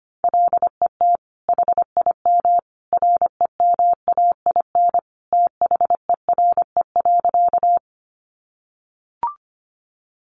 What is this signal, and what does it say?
Morse code, heard as a keyed tone, about 25 wpm, 700 Hz. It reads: LET HIM REMAIN THERE.
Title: Sentences from Top 1200 Words 25wpm